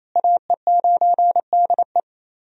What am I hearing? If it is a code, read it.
Morse code, heard as a keyed tone, about 27 wpm, 700 Hz. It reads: AE9DE
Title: USA Call Signs - 5 Characters - Repeated 2x - 28wpm